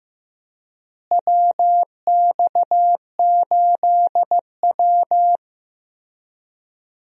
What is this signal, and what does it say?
Morse code, heard as a keyed tone, about 15 wpm, 700 Hz. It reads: WX8W
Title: USA Call Signs - 4 Characters - Repeated 3x - 15wpm